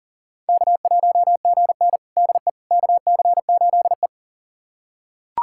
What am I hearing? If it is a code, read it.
Morse code, heard as a keyed tone, about 40 wpm, 700 Hz. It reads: K1GN DE KC8E
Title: USA Callsign DE Callsign - 4 Characters - 40wpm